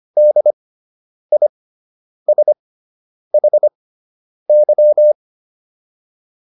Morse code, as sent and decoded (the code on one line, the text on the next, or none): -.. .. ... .... -.--
DISHY